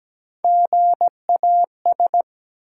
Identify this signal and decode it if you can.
Morse code, heard as a keyed tone, about 17 wpm, 700 Hz. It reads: GAS
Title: Three Letter Words - 17wpm